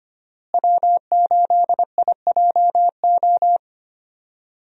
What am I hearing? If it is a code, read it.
Morse code, heard as a keyed tone, about 25 wpm, 700 Hz. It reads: W8IJO
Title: USA Call Signs - 5 Characters - Repeated 3x - 25wpm